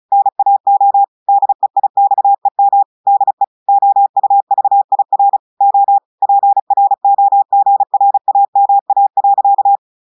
Morse code, heard as a keyed tone, about 35 wpm, 800 Hz. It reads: NAO DEIXEM DE OUVIR O PROGRAMA.
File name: portugais texte en clair 2 vitesse 35 mots minutes